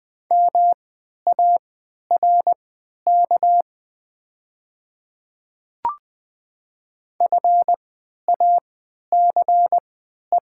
Morse code, heard as a keyed tone, about 20 wpm, spaced as for 11 wpm, 700 Hz. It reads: MARK FACE
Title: 3x Character Spacing - Top 500 Words - 20wpm